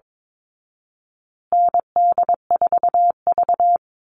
Morse code, heard as a keyed tone, about 22 wpm, 700 Hz. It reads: ND4V
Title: Call Signs Repeated 3x - 22wpm